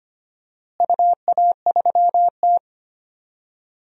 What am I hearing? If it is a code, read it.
Morse code, heard as a keyed tone, about 25 wpm, 700 Hz. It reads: UA3T
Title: DXCC-100 Call Signs - 4 Characters - Repeated 3x - 25wpm